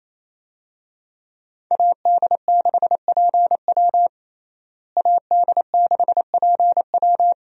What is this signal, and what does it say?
Morse code, heard as a keyed tone, about 28 wpm, 700 Hz. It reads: AD6PW AD6PW
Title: USA Call Signs - 5 Characters - Repeated 2x - 28wpm